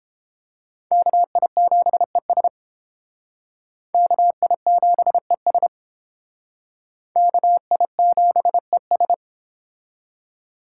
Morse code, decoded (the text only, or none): KI7ES KI7ES KI7ES